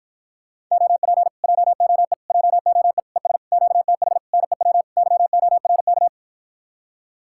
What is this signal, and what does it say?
Morse code, heard as a keyed tone, about 60 wpm, 700 Hz. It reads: OP JOE JOE ES QTH NEW YORK